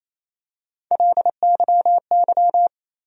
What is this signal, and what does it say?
Morse code, heard as a keyed tone, about 27 wpm, 700 Hz. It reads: LYY
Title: USA Call Sign Postfixes - 3 Letters 28wpm